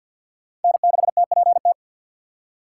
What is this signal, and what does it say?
Morse code, heard as a keyed tone, about 52 wpm, 700 Hz. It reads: N6TPT
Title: USA Call Signs - 5 Characters - 4x Character Spacing - 50wpm